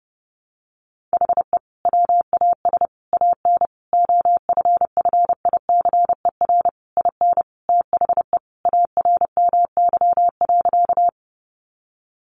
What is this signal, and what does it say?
Morse code, heard as a keyed tone, about 30 wpm, 700 Hz. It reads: HE WAS AN OFFICER IN THE ARMY.
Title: Sentences from Top 1200 Words 30wpm